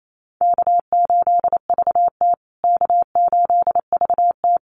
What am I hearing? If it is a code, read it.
Morse code, heard as a keyed tone, about 28 wpm, 700 Hz. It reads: K8VT K8VT
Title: USA Call Signs Repeated 2x 28wpm